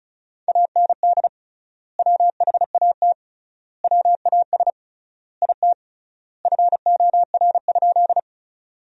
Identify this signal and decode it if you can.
Morse code, heard as a keyed tone, about 34 wpm, 700 Hz. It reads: AND WHAT WAS IT FOR?
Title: 3x Word-Spacing - Sentences from Top 100 Words - 35wpm